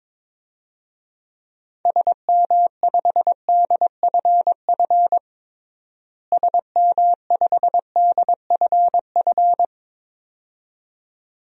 Morse code, heard as a keyed tone, about 22 wpm, 700 Hz. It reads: SM5DFF SM5DFF
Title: DXCC-100 Call Signs - 6 Characters - Repeated 2x - 22wpm